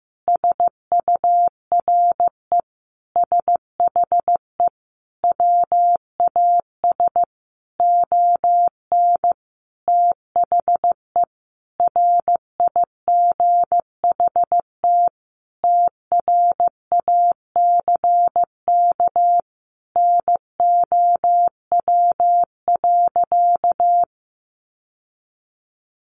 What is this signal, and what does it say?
Morse code, heard as a keyed tone, about 15 wpm, 700 Hz. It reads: SURE SHE WAS ON THE RIGHT TRACK NOW.
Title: Sentences from Top 1100 Words 15wpm